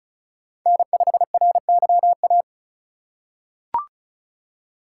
Morse code, heard as a keyed tone, about 36 wpm, 700 Hz. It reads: N5RYA
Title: USA Call Signs - 5 Characters - 3x Character Spacing - 35wpm